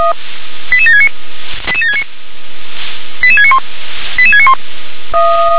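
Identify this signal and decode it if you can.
Five-tone selective call (ZVEI), heard as quick runs of five tones, tones of about 70 ms. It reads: 88769 88769 88691 88691
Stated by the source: ZVEI five-tone call received by SDR.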